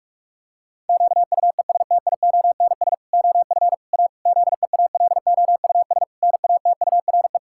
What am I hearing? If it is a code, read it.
Morse code, heard as a keyed tone, about 46 wpm, 700 Hz. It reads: QUESTIONS OF A ZEALOUS NATURE
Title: Pangram Challenge v2 - 45wpm